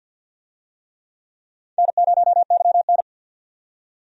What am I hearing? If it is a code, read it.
Morse code, heard as a keyed tone, about 47 wpm, 700 Hz. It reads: N0YN
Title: USA Call Signs - 4 Characters - 50wpm